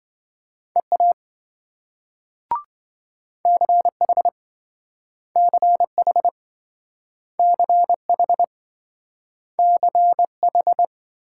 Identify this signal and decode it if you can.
Morse code, heard as a keyed tone, about 27 wpm, 700 Hz. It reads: EA CH CH CH CH